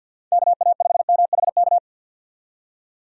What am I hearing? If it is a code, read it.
Morse code, heard as a keyed tone, about 50 wpm, 700 Hz. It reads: KA5MHK
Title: USA Call Signs - 6 Characters - 50wpm